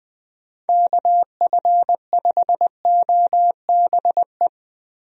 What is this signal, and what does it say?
Morse code, heard as a keyed tone, about 20 wpm, 700 Hz. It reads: KF5OBE
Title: USA Call Signs - 6 Characters - 20wpm